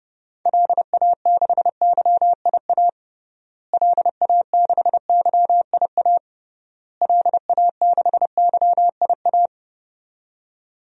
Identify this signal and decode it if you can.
Morse code, heard as a keyed tone, about 30 wpm, 700 Hz. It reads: LA6YIA LA6YIA LA6YIA
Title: DXCC-100 Call Signs - 6 Characters - Repeated 3x - 30wpm